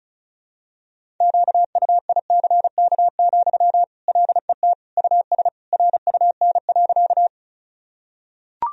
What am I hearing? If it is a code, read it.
Morse code, heard as a keyed tone, about 36 wpm, 700 Hz. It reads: QUICK, LET US RUN.
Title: Sentences from Top 1000 Words - Review Part 1 - 35wpm